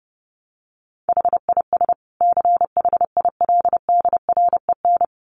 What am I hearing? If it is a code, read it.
Morse code, heard as a keyed tone, about 30 wpm, 700 Hz. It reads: HIS CHILDREN
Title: Sentences from Top 900 Words 30wpm